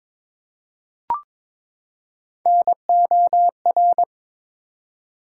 Morse code, heard as a keyed tone, about 22 wpm, 700 Hz. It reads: NOR
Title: Three Letter Words - 22wpm